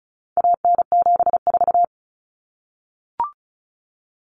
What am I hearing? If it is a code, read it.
Morse code, heard as a keyed tone, about 34 wpm, 700 Hz. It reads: AN74
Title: Common Cut Numbers - 4-Digit 35wpm